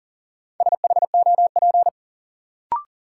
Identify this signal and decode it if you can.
Morse code, heard as a keyed tone, about 40 wpm, 700 Hz. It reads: SHOP
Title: Sentences from Top 1600 Words - New Word Drill - 40wpm